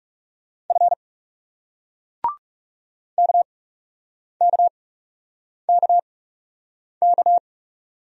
Morse code, heard as a keyed tone, about 40 wpm, spaced as for 8 wpm, 700 Hz. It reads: F KKKK